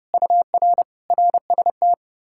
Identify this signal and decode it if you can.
Morse code, heard as a keyed tone, about 33 wpm, 700 Hz. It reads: UR RST